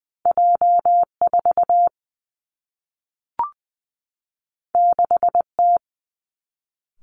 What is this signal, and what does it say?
Morse code, heard as a keyed tone, about 20 wpm, 700 Hz. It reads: J 4 6 T
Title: Two Challenging Letter-Number Combinations 20wpm